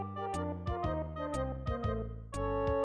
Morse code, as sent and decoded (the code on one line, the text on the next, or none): none